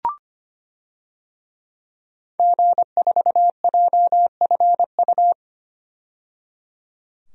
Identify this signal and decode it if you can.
Morse code, heard as a keyed tone, about 25 wpm, 700 Hz. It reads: G4JFU
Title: DXCC-100 Call Signs - 5 Characters - 25wpm